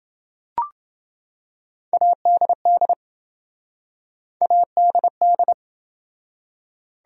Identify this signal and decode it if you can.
Morse code, heard as a keyed tone, about 26 wpm, 700 Hz. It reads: ADD ADD